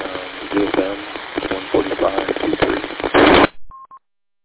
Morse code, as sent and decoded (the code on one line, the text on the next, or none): none